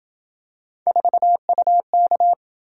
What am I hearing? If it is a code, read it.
Morse code, heard as a keyed tone, about 27 wpm, 700 Hz. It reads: 4UK